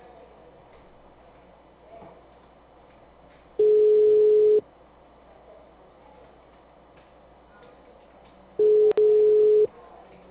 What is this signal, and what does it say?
Ringback tone (Europe, 425 Hz).